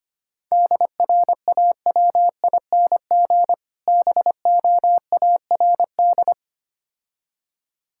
Morse code, heard as a keyed tone, about 25 wpm, 700 Hz. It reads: DRAWING BOARD
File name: practice / 25wpm / Difficult Words - Speed-Racing with Hints - 25wpm